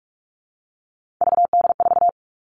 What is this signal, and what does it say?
Morse code, heard as a keyed tone, about 44 wpm, 700 Hz. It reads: VD4
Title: All Cut Numbers - 3-Digit 45wpm